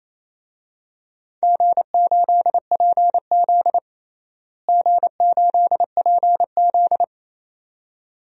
Morse code, heard as a keyed tone, about 27 wpm, 700 Hz. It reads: G8PZ G8PZ
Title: DXCC-100 Call Signs - 4 Characters - Repeated 2x - 28wpm